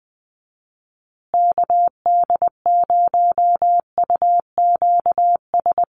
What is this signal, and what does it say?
Morse code, heard as a keyed tone, about 20 wpm, 700 Hz. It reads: KD0UQS
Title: USA Call Signs Repeated 3x 20wpm